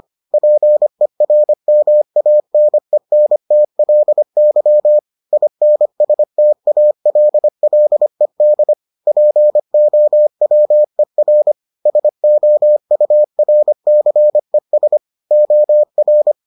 Morse code, heard as a keyed tone, about 25 wpm, 600 Hz. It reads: PERMANENTLY INSTALLED POWER SOURCES OR